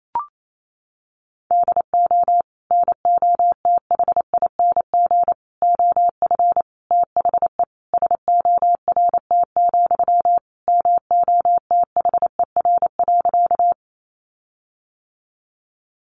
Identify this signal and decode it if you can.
Morse code, heard as a keyed tone, about 28 wpm, 700 Hz. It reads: DO NOTHING OF THE SORT, MOTHER.
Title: Sentences from Top 1300 Words 28wpm